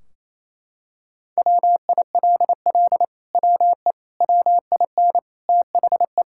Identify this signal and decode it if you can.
Morse code, heard as a keyed tone, about 28 wpm, 700 Hz. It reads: WILL WE WIN THE
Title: Sentences from Top 1600 Words 28wpm